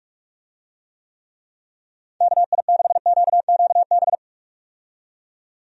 Morse code, heard as a keyed tone, about 42 wpm, 700 Hz. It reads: KI6QQB
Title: USA Call Signs - 6 Characters - 45wpm